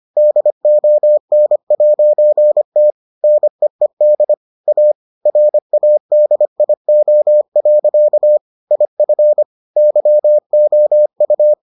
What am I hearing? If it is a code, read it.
Morse code, heard as a keyed tone, about 25 wpm, 600 Hz. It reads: DON'T NEED A RADIO. IF YOU